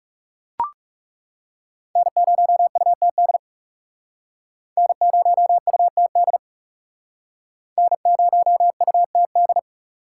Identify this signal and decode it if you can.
Morse code, heard as a keyed tone, about 37 wpm, 700 Hz. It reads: N0UTD N0UTD N0UTD
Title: Speed Racing - USA 1x3 Call Signs 30wpm